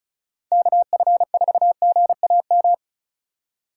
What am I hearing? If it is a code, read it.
Morse code, heard as a keyed tone, about 36 wpm, 690 Hz. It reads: KF4GAM
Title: USA Call Signs - 6 Characters - 2x Character Spacing - 35wpm